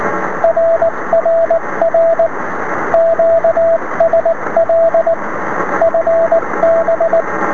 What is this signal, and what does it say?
Morse code, heard as a keyed tone, about 19 wpm, 660 Hz, with radio noise. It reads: RRR QSL FB
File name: sm7lcb / es5pc 23cm